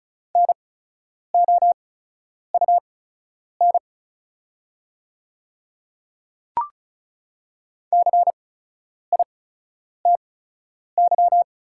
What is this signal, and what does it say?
Morse code, heard as a keyed tone, about 36 wpm, spaced as for 10 wpm, 690 Hz. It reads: NOUN CITY
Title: 8x Character Spacing - Top 500 Words - 35wpm